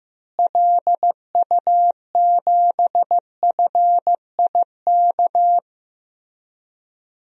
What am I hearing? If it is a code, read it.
Morse code, heard as a keyed tone, about 15 wpm, 700 Hz. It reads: LU7FIK
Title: DXCC-100 Call Signs - 6 Characters - Repeated 2x - 15wpm